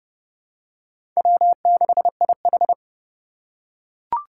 Morse code, heard as a keyed tone, about 32 wpm, 700 Hz. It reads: W6IH
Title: USA Call Signs - 4 Characters - 4x Character Spacing - 30wpm